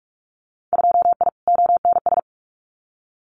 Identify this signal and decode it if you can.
Morse code, heard as a keyed tone, about 46 wpm, 700 Hz. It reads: 2I ONS
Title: Winter Field Day Exchanges Encore 45wpm